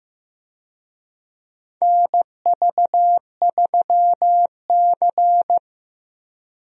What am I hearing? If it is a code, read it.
Morse code, heard as a keyed tone, about 15 wpm, 700 Hz. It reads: NV3C
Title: USA Call Signs - 4 Characters - Repeated 2x - 15wpm